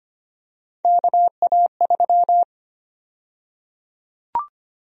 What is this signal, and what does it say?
Morse code, heard as a keyed tone, about 25 wpm, 700 Hz. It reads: KA3
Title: USA Call Sign Prefixes - 25wpm